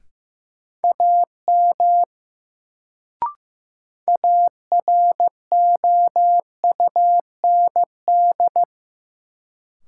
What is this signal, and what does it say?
Morse code, heard as a keyed tone, about 15 wpm, 700 Hz. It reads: AM AROUND
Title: Sentences from Top 600 Words - New Word Drill - 15wpm